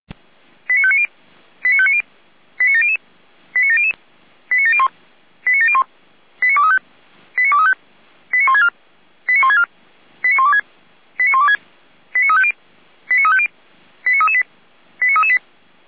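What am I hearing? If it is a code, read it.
Five-tone selective call (ZVEI), heard as quick runs of five tones, tones of about 70 ms. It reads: 89590 89590 89700 89700 89701 89701 89235 89235 89164 89164 89126 89126 89350 89350 89338 89338
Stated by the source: ZVEI five-tone selective call received on an SDR.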